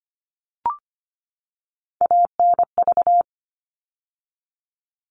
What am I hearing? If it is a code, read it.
Morse code, heard as a keyed tone, about 25 wpm, 700 Hz. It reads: ANV
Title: All Cut Numbers - 3-Digit 25wpm